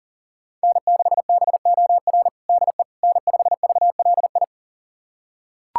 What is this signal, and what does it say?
Morse code, heard as a keyed tone, about 37 wpm, 700 Hz. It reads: N6BOR DE N5VLI
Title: USA Callsign DE Callsign - 5 Characters - 40wpm